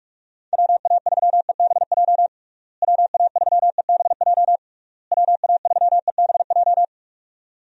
Morse code, heard as a keyed tone, about 41 wpm, 700 Hz. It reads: WA3EBJ WA3EBJ WA3EBJ
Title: USA Call Signs - 6 Characters - Repeated 3x - 45wpm